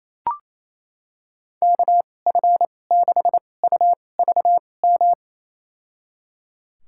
Morse code, heard as a keyed tone, about 28 wpm, spaced as for 20 wpm, 700 Hz. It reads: KF6UVM
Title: USA Call Signs - 6 Characters - 2x Character Spacing - 28wpm